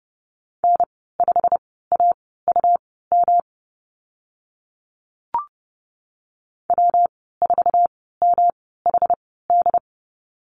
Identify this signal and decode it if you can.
Morse code, heard as a keyed tone, about 30 wpm, spaced as for 17 wpm, 700 Hz. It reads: N5AUM W4MHD
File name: Lesson 19g - Letter w - Call Signs 30wpm